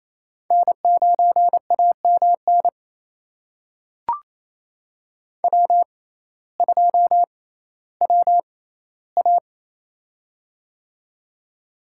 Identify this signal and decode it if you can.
Morse code, heard as a keyed tone, about 28 wpm, spaced as for 15 wpm, 700 Hz. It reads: N9AMN W2WA
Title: USA Call Signs - 6x Character Spacing - 28wpm